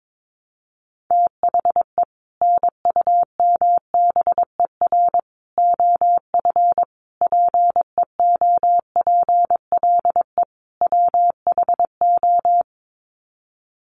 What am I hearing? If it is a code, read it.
Morse code, heard as a keyed tone, about 22 wpm, 700 Hz. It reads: THE NUMBER OF PEOPLE WHO
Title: Sets of 5 Words Encore 22wpm